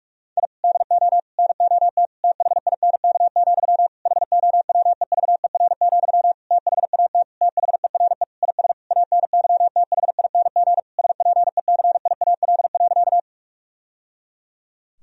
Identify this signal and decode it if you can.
Morse code, heard as a keyed tone, about 43 wpm, 700 Hz. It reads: I DO NOT THINK, HOWEVER, THAT THERE IS ANYTHING SPECIAL.
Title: Sentences from Top 1000 Words - Review Part 3 - 45wpm